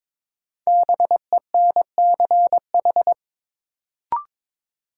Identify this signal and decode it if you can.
Morse code, heard as a keyed tone, about 23 wpm, 700 Hz. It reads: BENCH
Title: Sentences from Top 1400 Words - New Word Drill Alt - 22wpm